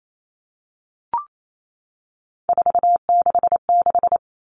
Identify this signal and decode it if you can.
Morse code, heard as a keyed tone, about 28 wpm, 700 Hz. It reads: 466
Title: Numbers - 3-Digit 28wpm